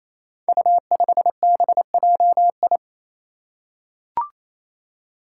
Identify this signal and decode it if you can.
Morse code, heard as a keyed tone, about 28 wpm, 700 Hz. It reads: U5BJI
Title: Five Challenging Letter-Number Combinations 28wpm